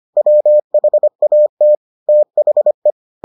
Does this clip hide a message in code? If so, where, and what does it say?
Yes: Morse code from 0.2 to 2.9 s, heard as a keyed tone, about 25 wpm, 600 Hz. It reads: WHAT THE